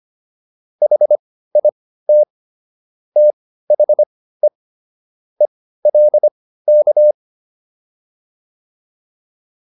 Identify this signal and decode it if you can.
Morse code, heard as a keyed tone, about 25 wpm, spaced as for 15 wpm, 600 Hz. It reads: HIT THE ELK